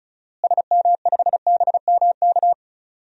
Morse code, heard as a keyed tone, about 36 wpm, 700 Hz. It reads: SM5BMK
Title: DXCC-100 Call Signs - 6 Characters - Repeated 2x - 35wpm